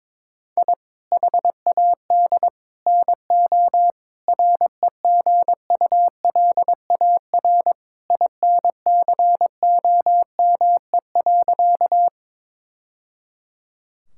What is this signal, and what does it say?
Morse code, heard as a keyed tone, about 22 wpm, 700 Hz. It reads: I HAD NO REGULAR INCOME.
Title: Sentences from Top 1000 Words - Review Part 6 - 22wpm